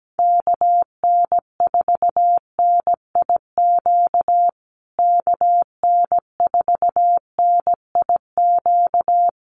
Morse code, heard as a keyed tone, about 17 wpm, 700 Hz. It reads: KN4NIQ KN4NIQ
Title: USA Call Signs Repeated 3x 17wpm